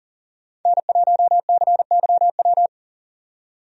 Morse code, heard as a keyed tone, about 40 wpm, 700 Hz. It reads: N1CYW
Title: USA Call Signs - 5 Characters - 40wpm